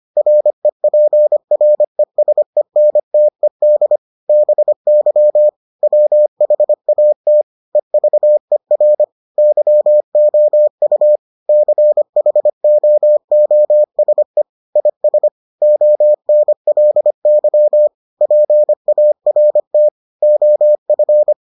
Morse code, heard as a keyed tone, about 25 wpm, 600 Hz. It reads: REPRESENTED BY WHAT EVER YOU CHOOSE IS ONLY PART OF